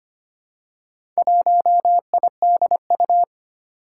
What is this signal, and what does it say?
Morse code, heard as a keyed tone, about 25 wpm, 700 Hz. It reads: 1IDU
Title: Four Challenging Letter-Number Combinations 25wpm